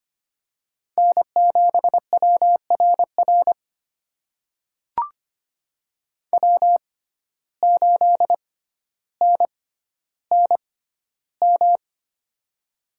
Morse code, heard as a keyed tone, about 25 wpm, spaced as for 13 wpm, 700 Hz. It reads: N7WRR W8NNM
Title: USA Call Signs - 5 Characters - 6x Character Spacing - 25wpm